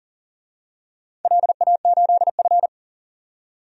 Morse code, heard as a keyed tone, about 40 wpm, 700 Hz. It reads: LA8F
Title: DXCC-100 Call Signs - 4 Characters - 40wpm